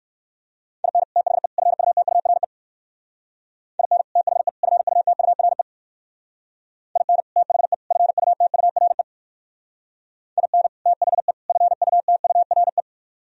Morse code, heard as a keyed tone, about 58 wpm, 700 Hz. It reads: IN THE FUTURE IN THE FUTURE IN THE FUTURE IN THE FUTURE